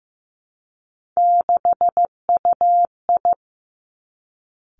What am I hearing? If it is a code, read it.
Morse code, heard as a keyed tone, about 15 wpm, 700 Hz. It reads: 6UI
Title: Three Challenging Letter-Number Combinations 15wpm